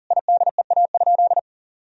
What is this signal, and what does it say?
Morse code, heard as a keyed tone, about 37 wpm, 700 Hz. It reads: IDEA?
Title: Sentences from Top 1000 Words - Review Part 4 - 40wpm